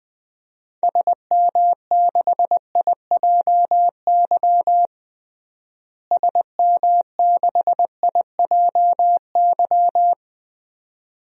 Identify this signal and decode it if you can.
Morse code, heard as a keyed tone, about 21 wpm, 700 Hz. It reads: SM6IJY SM6IJY
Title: DXCC-100 Call Signs - 6 Characters - Repeated 2x - 20wpm